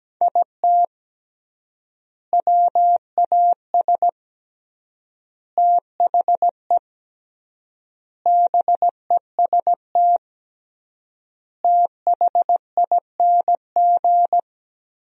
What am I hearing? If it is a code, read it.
Morse code, heard as a keyed tone, about 17 wpm, 700 Hz. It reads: IT WAS THE BEST THING
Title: 3x Word-Spacing - Sentences from Top 400 Words - 17wpm